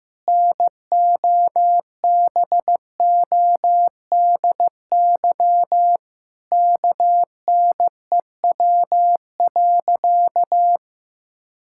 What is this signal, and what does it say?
Morse code, heard as a keyed tone, about 16 wpm, 700 Hz. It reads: NOBODY KNEW.
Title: Sentences from Top 1600 Words 15wpm